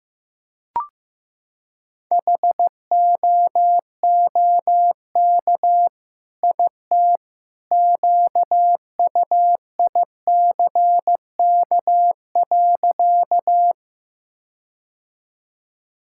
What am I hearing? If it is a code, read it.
Morse code, heard as a keyed tone, about 15 wpm, 700 Hz. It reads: HOOK IT QUICK.
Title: Sentences from Top 1600 Words 15wpm